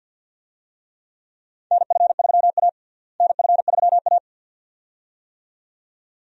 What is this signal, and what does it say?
Morse code, heard as a keyed tone, about 51 wpm, 700 Hz. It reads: NF3A NF3A
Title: USA Call Signs - 4 Characters - Repeated 2x - 50wpm